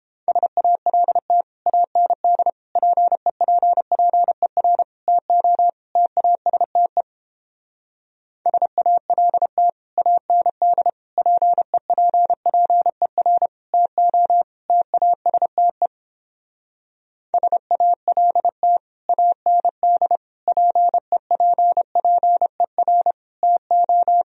Morse code, heard as a keyed tone, about 31 wpm, 700 Hz. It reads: SALT AND PEPPER TO TASTE SALT AND PEPPER TO TASTE SALT AND PEPPER TO